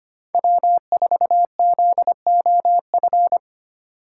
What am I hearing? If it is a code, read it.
Morse code, heard as a keyed tone, about 25 wpm, 700 Hz. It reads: W4ZOF
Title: Code Groups - 5 Letter:Numbers - 25wpm